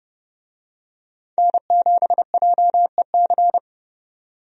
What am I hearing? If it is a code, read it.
Morse code, heard as a keyed tone, about 31 wpm, 700 Hz. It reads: N7JEC
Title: USA Call Signs - 5 Characters - 30wpm